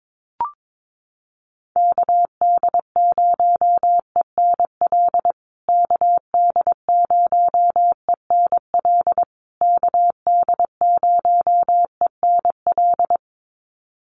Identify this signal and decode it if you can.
Morse code, heard as a keyed tone, about 22 wpm, 700 Hz. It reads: KD0ENL KD0ENL KD0ENL
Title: USA Call Signs Repeated 3x 22wpm